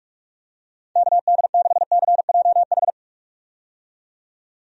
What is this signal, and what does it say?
Morse code, heard as a keyed tone, about 44 wpm, 700 Hz. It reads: KD6CJH
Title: USA Call Signs - 6 Characters - 45wpm